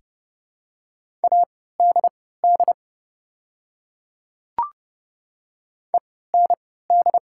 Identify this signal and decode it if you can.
Morse code, heard as a keyed tone, about 30 wpm, spaced as for 17 wpm, 690 Hz. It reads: ADD END
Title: 3x Character Spacing - Top 200 Words - 30wpm